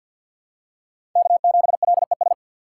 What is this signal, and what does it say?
Morse code, heard as a keyed tone, about 51 wpm, 700 Hz. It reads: K7LES
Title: USA Call Signs - 6x Character Spacing - 50wpm